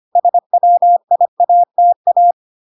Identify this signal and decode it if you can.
Morse code, heard as a keyed tone, about 25 wpm, 700 Hz. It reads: SWIATA